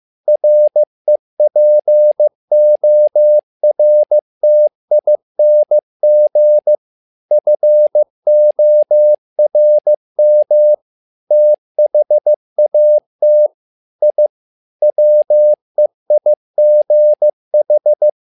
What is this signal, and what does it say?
Morse code, heard as a keyed tone, about 15 wpm, 600 Hz. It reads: REPORTING FORM THAT I WEIGH
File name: learning center / tower trouble / tower trouble 15wpm